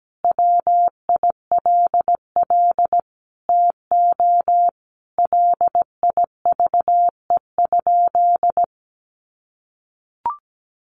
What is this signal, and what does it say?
Morse code, heard as a keyed tone, about 17 wpm, 700 Hz. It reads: WILL TO LIVE?
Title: Sentences from Top 1500 Words 17wpm